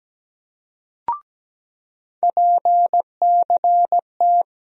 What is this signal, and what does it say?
Morse code, heard as a keyed tone, about 17 wpm, 700 Hz. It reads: PCT
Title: USA Call Sign Postfixes - 3 Letters 17wpm